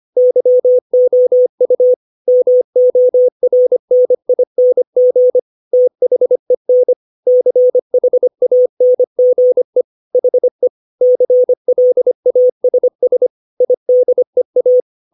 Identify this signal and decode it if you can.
Morse code, heard as a keyed tone, about 25 wpm, 500 Hz. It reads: YOU MORNING THEN CHANGE HE CLASS IDEA
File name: 632 words / 632 Words 25WPM